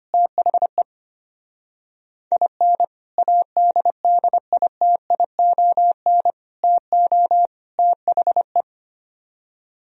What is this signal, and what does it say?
Morse code, heard as a keyed tone, about 30 wpm, 700 Hz. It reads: THE IN ADDITION TO THE
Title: Speed-Racing - Sets of 4 Words 25wpm